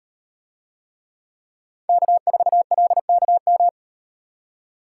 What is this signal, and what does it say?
Morse code, heard as a keyed tone, about 38 wpm, 700 Hz. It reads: K4LKM